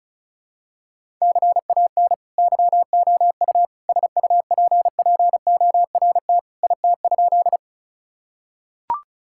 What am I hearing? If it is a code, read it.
Morse code, heard as a keyed tone, about 34 wpm, 700 Hz. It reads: CAN YOU SUPPORT IT?
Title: Sentences from Top 1000 Words - Review Part 5 - 35wpm